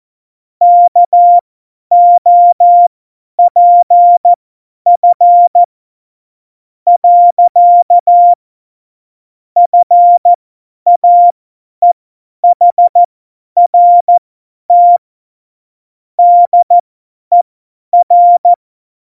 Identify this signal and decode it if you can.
Morse code, heard as a keyed tone, about 14 wpm, spaced as for 9 wpm, 700 Hz. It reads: KOPF. FAEHRT DER
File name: Morsekurs / Witze / Witz 13 March 2018, 50 F70